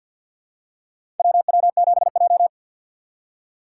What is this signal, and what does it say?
Morse code, heard as a keyed tone, about 49 wpm, 700 Hz. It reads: WW7J
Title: USA Call Signs - 4 Characters - 6x Character Spacing - 50wpm